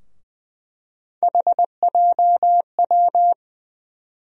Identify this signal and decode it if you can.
Morse code, heard as a keyed tone, about 21 wpm, 700 Hz. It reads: HJW
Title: USA Call Sign Postfixes - 3 Letters 20wpm